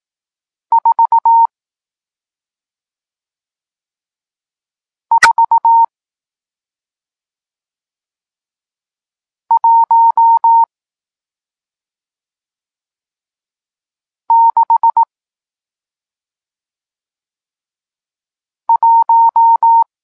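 Morse code, heard as a keyed tone, about 18 wpm, spaced as for 2 wpm, 920 Hz. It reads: 44161
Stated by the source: Morse code lesson teaching digits 1, 4 and 6, from SP9MOA, Niepołomicki Klub Krótkofalowców (Poland) amateur radio club.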